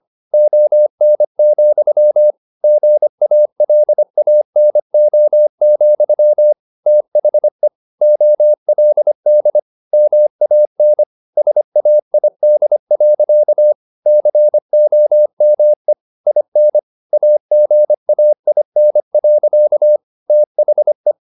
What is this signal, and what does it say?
Morse code, heard as a keyed tone, about 25 wpm, 600 Hz. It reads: ON, GALANO, THE OLD MAN SAID. COME IN AGAIN. THE